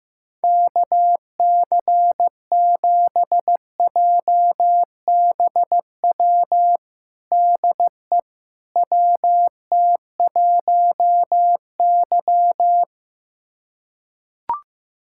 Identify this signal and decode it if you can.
Morse code, heard as a keyed tone, about 15 wpm, 700 Hz. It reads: KC7JBW DE WT1Y
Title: USA Callsign DE Callsign 15wpm